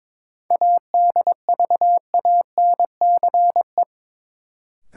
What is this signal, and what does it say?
Morse code, heard as a keyed tone, about 23 wpm, 700 Hz. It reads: ADVANCE
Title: Sentences from Top 1400 Words - New Word Drill - 22wpm